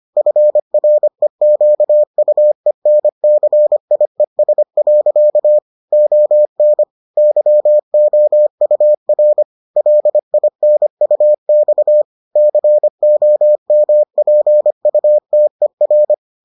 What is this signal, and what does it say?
Morse code, heard as a keyed tone, about 25 wpm, 600 Hz. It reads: FREQUENCIES. ON YOUR LINUX COMPUTER